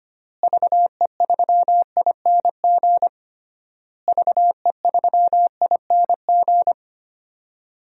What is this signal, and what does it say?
Morse code, heard as a keyed tone, about 25 wpm, 700 Hz. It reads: VE3ING VE3ING
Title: DXCC-100 Call Signs - 6 Characters - Repeated 3x - 25wpm